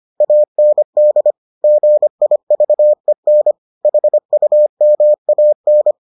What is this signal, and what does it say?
Morse code, heard as a keyed tone, about 25 wpm, 600 Hz. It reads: AND GIVEN HUMAN